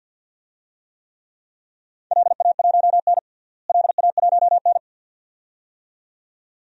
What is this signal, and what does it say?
Morse code, heard as a keyed tone, about 52 wpm, 700 Hz. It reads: LA1N LA1N
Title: DXCC-100 Call Signs - 4 Characters - Repeated 2x - 50wpm